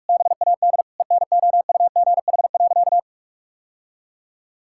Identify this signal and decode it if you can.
Morse code, heard as a keyed tone, about 46 wpm, 700 Hz. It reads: BAD ENOUGH.
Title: Sentences from Top 1000 Words - Review Part 5 - 45wpm